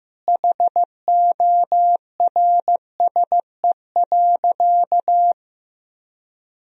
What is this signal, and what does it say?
Morse code, heard as a keyed tone, about 16 wpm, 700 Hz. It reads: HORSE.
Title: Sentences from Top 1000 Words - Review Part 4 - 15wpm